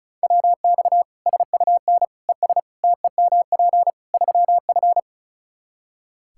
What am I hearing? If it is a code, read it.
Morse code, heard as a keyed tone, about 34 wpm, 700 Hz. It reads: WX SUN ES TEMP 3F
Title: K1USN SST QSO Practice 35wpm